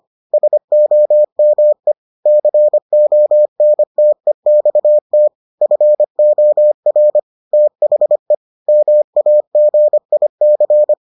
Morse code, heard as a keyed tone, about 25 wpm, 600 Hz. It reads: SOME CONTEXT FOR THE MAGIC